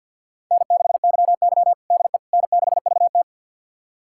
Morse code, heard as a keyed tone, about 50 wpm, 700 Hz. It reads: N6YY DE N6VT
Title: USA Callsign DE Callsign - 4 Characters - 50wpm